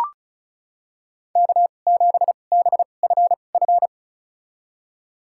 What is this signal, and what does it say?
Morse code, heard as a keyed tone, about 36 wpm, spaced as for 25 wpm, 700 Hz. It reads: K7BFF
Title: USA Call Signs - 5 Characters - 2x Character Spacing - 35wpm